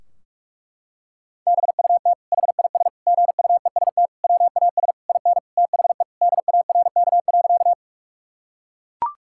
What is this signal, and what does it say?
Morse code, heard as a keyed tone, about 41 wpm, 700 Hz. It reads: BUT HIS GUEST WAS IN THE DARK.
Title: Sentences from Top 1600 Words 45wpm